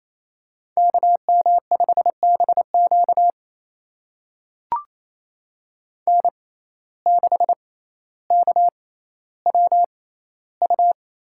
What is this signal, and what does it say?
Morse code, heard as a keyed tone, about 28 wpm, spaced as for 14 wpm, 700 Hz. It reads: KM5BQ N6KWU
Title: USA Call Signs - 5 Characters - 6x Character Spacing - 28wpm